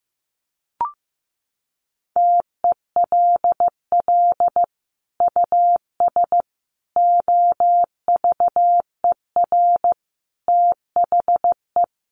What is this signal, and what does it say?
Morse code, heard as a keyed tone, about 15 wpm, 700 Hz. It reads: TELL US OVER THE
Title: Sentences from Top 1100 Words 15wpm